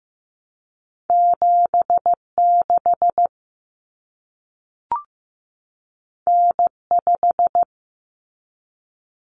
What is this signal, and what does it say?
Morse code, heard as a keyed tone, about 15 wpm, 700 Hz. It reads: 76 N5
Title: Common Cut Numbers - 2-Digit 15wpm